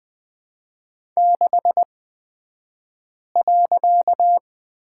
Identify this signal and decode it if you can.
Morse code, heard as a keyed tone, about 20 wpm, spaced as for 5 wpm, 700 Hz. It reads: 6.
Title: 254 Lesson 41d - Period - Two Characters 20wpm